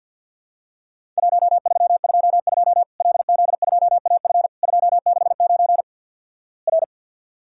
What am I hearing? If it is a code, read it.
Morse code, heard as a keyed tone, about 50 wpm, 700 Hz. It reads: 1322 LZ2AF 269 R
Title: Worked All Europe DX Contest - QTC Exchange with Repeats - 50wpm